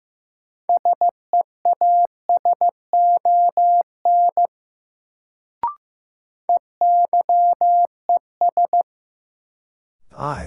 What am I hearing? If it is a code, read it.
Morse code, heard as a keyed tone, about 15 wpm, 700 Hz. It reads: SEASON EYES
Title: Sets of 5 Words - New Word Drill - 15wpm